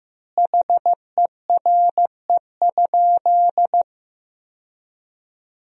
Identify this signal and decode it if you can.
Morse code, heard as a keyed tone, about 15 wpm, 700 Hz. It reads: HERE?
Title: 4x Word-Spacing - Sentences from Top 200 Words - 15wpm